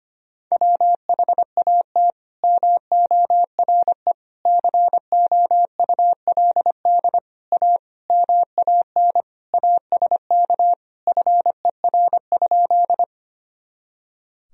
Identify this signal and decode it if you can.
Morse code, heard as a keyed tone, about 25 wpm, 700 Hz. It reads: WHAT MORE COULD A MAN ASK FER?
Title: Sentences from Top 200 Words and Abbreviations 25wpm